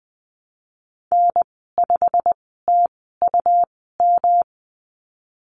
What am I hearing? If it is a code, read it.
Morse code, heard as a keyed tone, about 19 wpm, spaced as for 14 wpm, 700 Hz. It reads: N5TUM
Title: 140 Lesson 23g - Letter f - Call Signs 20wpm